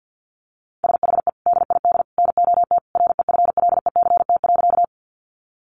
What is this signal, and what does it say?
Morse code, heard as a keyed tone, about 50 wpm, 700 Hz. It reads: SHE DID NOT REFLECT.
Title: Sentences from Top 1500 Words 50wpm